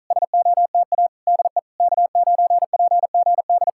Morse code, heard as a keyed tone, about 46 wpm, 700 Hz. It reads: SOTA DE K9PGD